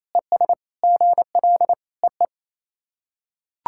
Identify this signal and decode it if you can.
Morse code, heard as a keyed tone, about 28 wpm, 690 Hz. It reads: ES GL EE
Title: SOTA QSO Practice 28wpm